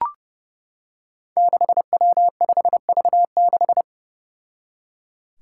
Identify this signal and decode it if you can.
Morse code, heard as a keyed tone, about 30 wpm, 700 Hz. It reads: 6W5V6
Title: Five Challenging Letter-Number Combinations 30wpm